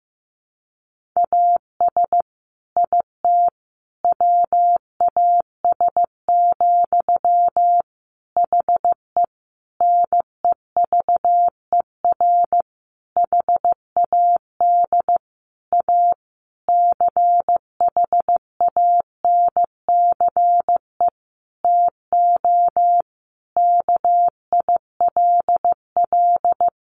Morse code, heard as a keyed tone, about 15 wpm, 700 Hz. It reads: AS IT WAS, HE NEVER HAD A CHANCE TO KILL
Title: Sentences from Top 1400 Words 15wpm